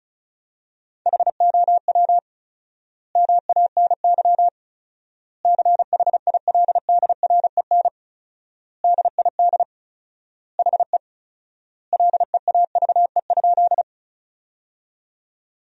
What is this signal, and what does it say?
Morse code, heard as a keyed tone, about 36 wpm, 700 Hz. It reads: HOW MANY CHILDREN DID HE LEAVE?
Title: 4x Word-Spacing - Sentences from Top 400 Words - 35wpm